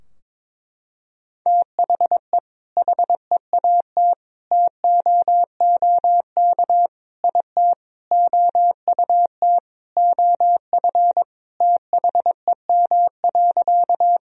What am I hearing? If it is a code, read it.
Morse code, heard as a keyed tone, about 22 wpm, 700 Hz. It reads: THE HEAT TOOK IT OUT OF THEM.
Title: Sentences from Top 1000 Words - Review Part 3 - 22wpm